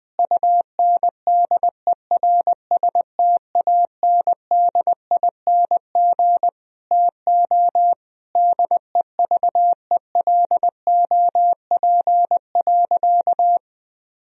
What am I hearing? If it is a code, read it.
Morse code, heard as a keyed tone, about 21 wpm, 700 Hz. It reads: UNDERSTANDING TO DEVELOP.
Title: Sentences from Top 1000 Words - Review Part 3 - 20wpm